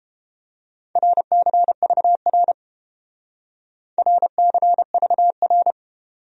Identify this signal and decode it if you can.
Morse code, heard as a keyed tone, about 31 wpm, 700 Hz. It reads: RCVR RCVR